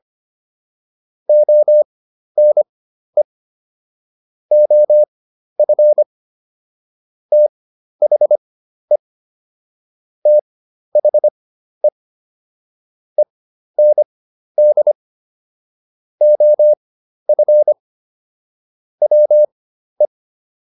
Morse code, heard as a keyed tone, about 25 wpm, spaced as for 12 wpm, 600 Hz. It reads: ONE OF THE THE END OF WE